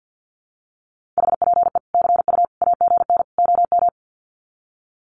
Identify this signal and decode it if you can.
Morse code, heard as a keyed tone, about 50 wpm, 700 Hz. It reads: HPE CU AGN OM
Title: Common QSO Phrases 50wpm